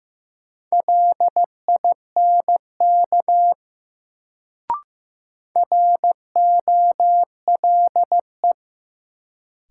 Morse code, heard as a keyed tone, about 15 wpm, 700 Hz. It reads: LINK ROLE
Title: Sentences from Top 1500 Words - New Word Drill Alt - 15wpm